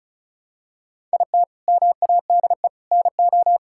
Morse code, heard as a keyed tone, about 36 wpm, 700 Hz. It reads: IT MADE NO
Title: Sentences from Top 1000 Words - Review Part 5 - 35wpm